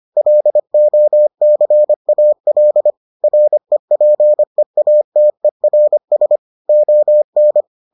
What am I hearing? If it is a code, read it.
Morse code, heard as a keyed tone, about 25 wpm, 600 Hz. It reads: LOCAL REPEATERS ON